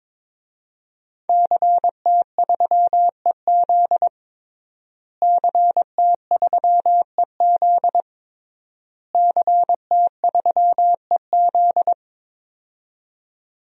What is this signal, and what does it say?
Morse code, heard as a keyed tone, about 22 wpm, 700 Hz. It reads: CT3EZ CT3EZ CT3EZ
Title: DXCC-100 Call Signs - 5 Characters - Repeated 3x - 22wpm